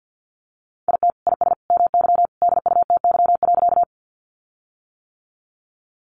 Morse code, heard as a keyed tone, about 50 wpm, 700 Hz. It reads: IT IS MY DUTY.
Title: Sentences from Top 1500 Words 50wpm